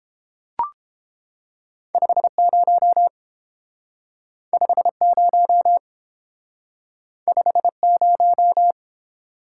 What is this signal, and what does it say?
Morse code, heard as a keyed tone, about 29 wpm, 700 Hz. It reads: 50 50 50